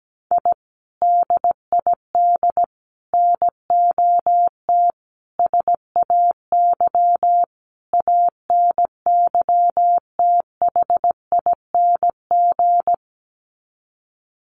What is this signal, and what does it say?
Morse code, heard as a keyed tone, about 17 wpm, 700 Hz. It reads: I DID NOT SAY ANYTHING
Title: Sets of 5 Words Encore2 17wpm